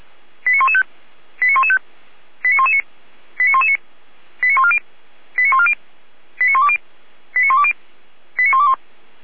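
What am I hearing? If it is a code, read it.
Five-tone selective call (ZVEI), heard as quick runs of five tones, tones of about 70 ms. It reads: 89115 89115 89119 89119 89140 89140 89120 89120 89121
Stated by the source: Five-tone ZVEI selective call received on SDR by Feuerwehr Orlamünde.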